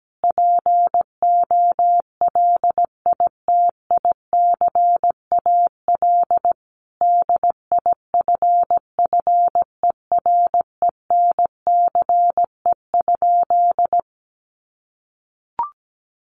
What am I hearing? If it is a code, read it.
Morse code, heard as a keyed tone, about 17 wpm, 700 Hz. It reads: POLITICAL DIFFERENCE?
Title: Sentences from Top 800 Words 17wpm